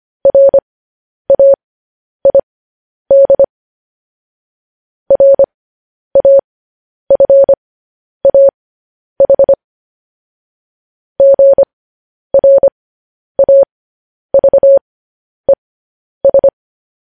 Morse code, heard as a keyed tone, about 25 wpm, spaced as for 10 wpm, 550 Hz. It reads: RAID RAFAH GRAVES